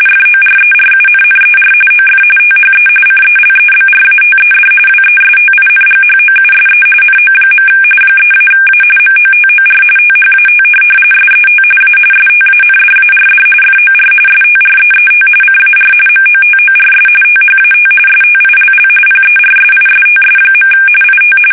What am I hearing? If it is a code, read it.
RTTY, about 45.45 baud. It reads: FORERUNNER OF THE CELEBRATED WESTINGHOUSE NO. 3-A MOTOR THAT SET A STANDARD FOR STREET CAR MOTORS THAT STILL PERSISTS. IT
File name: fsk850 / benjamin-g-lamme